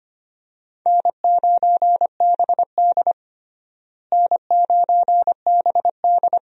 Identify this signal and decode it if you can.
Morse code, heard as a keyed tone, about 25 wpm, 700 Hz. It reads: N9BD N9BD
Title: USA Call Signs - 4 Characters - Repeated 2x - 25wpm